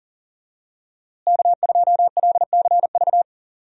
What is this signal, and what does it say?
Morse code, heard as a keyed tone, about 39 wpm, 700 Hz. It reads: K2LCV
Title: USA Call Signs - 5 Characters - Repeated 2x - 40wpm